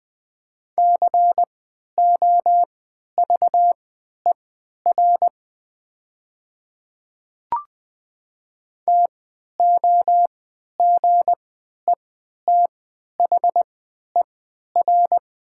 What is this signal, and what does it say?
Morse code, heard as a keyed tone, about 20 wpm, spaced as for 11 wpm, 700 Hz. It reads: COVER TOGETHER
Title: 3x Character Spacing - Top 300 Words - 20wpm